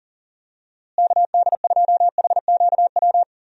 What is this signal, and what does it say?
Morse code, heard as a keyed tone, about 40 wpm, 700 Hz. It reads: KD2HQW
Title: USA Call Signs - 6 Characters - Repeated 2x - 40wpm